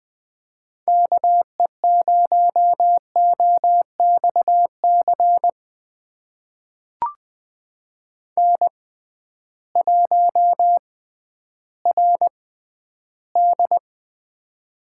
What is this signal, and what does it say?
Morse code, heard as a keyed tone, about 21 wpm, spaced as for 12 wpm, 700 Hz. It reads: KE0OXC N1RD
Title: USA Call Signs - 6x Character Spacing - 20wpm